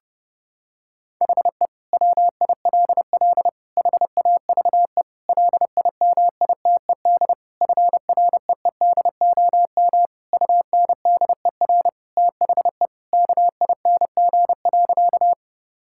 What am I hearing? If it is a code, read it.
Morse code, heard as a keyed tone, about 32 wpm, 700 Hz. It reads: HE WILL HAVE LIMITED FREEDOM UNDER THE KING.
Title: Sentences from Top 1000 Words - Review Part 6 - 30wpm